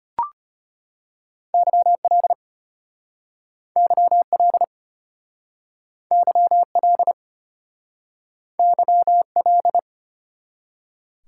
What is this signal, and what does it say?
Morse code, heard as a keyed tone, about 33 wpm, 700 Hz. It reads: YL YL YL YL